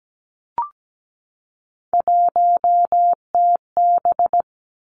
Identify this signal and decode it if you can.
Morse code, heard as a keyed tone, about 17 wpm, 700 Hz. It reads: 1TB
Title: Three Challenging Letter-Number Combinations 17wpm